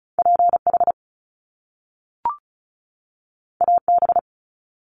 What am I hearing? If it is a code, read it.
Morse code, heard as a keyed tone, about 36 wpm, 700 Hz. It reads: P H A B
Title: Top Two Letter Combinations in Words 35wpm